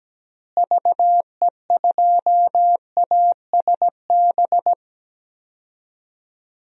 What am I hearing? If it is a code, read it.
Morse code, heard as a keyed tone, about 17 wpm, 700 Hz. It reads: VE2ASB
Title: DXCC-100 Call Signs - 6 Characters - Repeated 3x - 17wpm